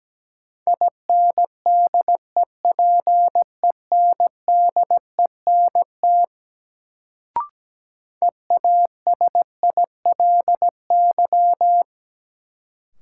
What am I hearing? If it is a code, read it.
Morse code, heard as a keyed tone, about 17 wpm, 700 Hz. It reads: INDEPENDENT EASILY
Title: Sentences from Top 900 Words - New Word Drill - 17wpm